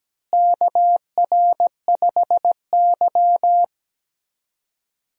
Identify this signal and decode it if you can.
Morse code, heard as a keyed tone, about 17 wpm, 700 Hz. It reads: KR5Y
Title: USA Call Signs - 4 Characters - Repeated 2x - 17wpm